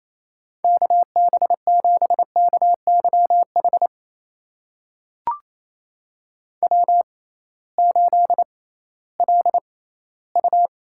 Morse code, heard as a keyed tone, about 27 wpm, spaced as for 16 wpm, 700 Hz. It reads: KB7KYH W8LU
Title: USA Call Signs - 6x Character Spacing - 28wpm